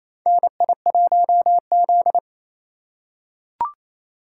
Morse code, heard as a keyed tone, about 28 wpm, 700 Hz. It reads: NI1Z
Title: USA Call Signs - 4 Characters - 6x Character Spacing - 28wpm